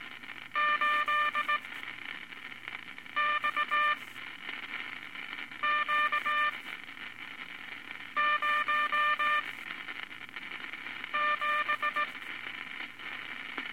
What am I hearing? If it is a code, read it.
Morse code, heard as a keyed tone, about 18 wpm, spaced as for 5 wpm, 1190 Hz, with noise. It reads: 8XQ07
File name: Morse Code Records/15A LCM 158 Units I-V Groups of Five